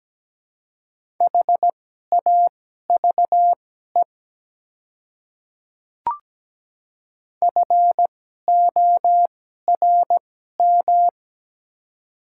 Morse code, heard as a keyed tone, about 17 wpm, spaced as for 12 wpm, 700 Hz. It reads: HAVE FORM
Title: 2x Character Spacing - Top 200 Words - 17wpm